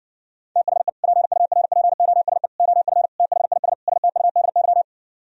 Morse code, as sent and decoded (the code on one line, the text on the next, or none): - .... / .--. ..- .-. .--. --- ... / --- ..-. / - .... .. ... / ... - ..- -.. -.--
THE PURPOSE OF THIS STUDY